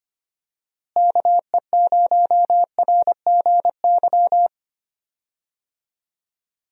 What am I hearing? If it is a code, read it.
Morse code, heard as a keyed tone, about 25 wpm, 700 Hz. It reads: KE0RGY
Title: USA Call Signs - 6 Characters - Repeated 2x - 25wpm